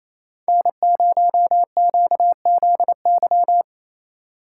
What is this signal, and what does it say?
Morse code, heard as a keyed tone, about 27 wpm, 690 Hz. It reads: N0QZY
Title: USA Call Signs - 3x Character Spacing - 28wpm